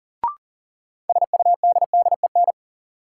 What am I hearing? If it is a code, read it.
Morse code, heard as a keyed tone, about 40 wpm, 700 Hz. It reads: SUDDEN
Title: Sets of 4 Words - New Word Drill - 40wpm